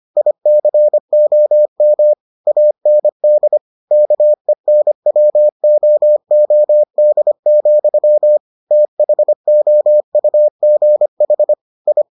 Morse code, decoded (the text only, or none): ICOM AND KENWOOD, THOUGH I